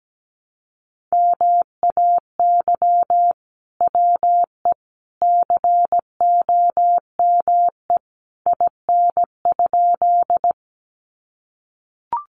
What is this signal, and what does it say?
Morse code, heard as a keyed tone, about 17 wpm, 700 Hz. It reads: MAY WE COME IN?
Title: Sentences from Top 100 Words 17wpm